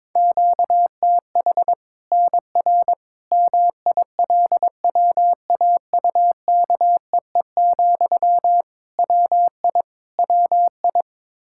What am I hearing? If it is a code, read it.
Morse code, heard as a keyed tone, about 22 wpm, 700 Hz. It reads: QTH NR MILWAUKEE, WI WI
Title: K1USN SST QSO Practice 22wpm